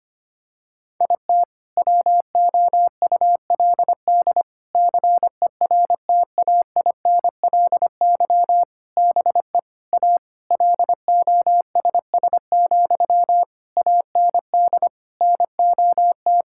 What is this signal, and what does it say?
Morse code, heard as a keyed tone, about 25 wpm, 700 Hz. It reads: IT WOULD CERTAINLY BE A LOSS, AND NOT
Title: Sentences from Top 1600 Words 25wpm